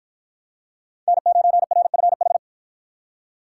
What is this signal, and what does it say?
Morse code, heard as a keyed tone, about 48 wpm, 700 Hz. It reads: N9RFH